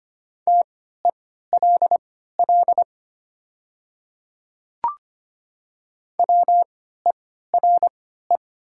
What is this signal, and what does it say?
Morse code, heard as a keyed tone, about 25 wpm, spaced as for 14 wpm, 700 Hz. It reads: TELL WERE
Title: 3x Character Spacing - Top 300 Words - 25wpm